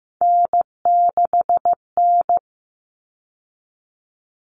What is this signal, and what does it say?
Morse code, heard as a keyed tone, about 15 wpm, 700 Hz. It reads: N6N
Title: All Cut Numbers - 3-Digit 15wpm